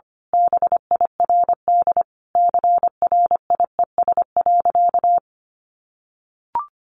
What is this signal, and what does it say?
Morse code, heard as a keyed tone, about 25 wpm, 700 Hz. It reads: BIRD CRIES.
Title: A Sound Of Thunder 25wpm